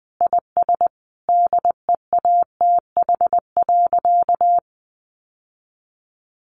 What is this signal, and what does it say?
Morse code, heard as a keyed tone, about 20 wpm, 700 Hz. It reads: IS DEATH.
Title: Sentences from Top 1100 Words 20wpm